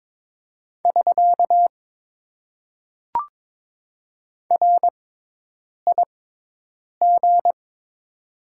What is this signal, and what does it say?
Morse code, heard as a keyed tone, about 22 wpm, spaced as for 8 wpm, 700 Hz. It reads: <SK> RIG
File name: Top 90 QSO Elements - 6x Character Spacing - 22wpm